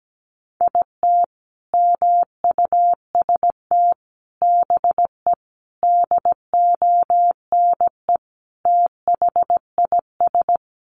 Morse code, heard as a keyed tone, about 17 wpm, 700 Hz. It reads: IT MUST BE DONE THIS
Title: Sentences from Top 1500 Words 17wpm